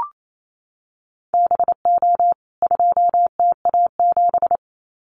Morse code, heard as a keyed tone, about 28 wpm, 700 Hz. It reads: BO 2TA7
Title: CWT Exchange 28wpm